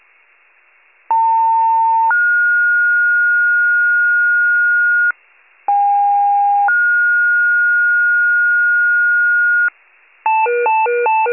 Two-tone page, heard as a one-second tone, then a three-second tone, about 900 then 1430 Hz, about 810 then 1430 Hz. It is heard from 1.1 to 9.7 s.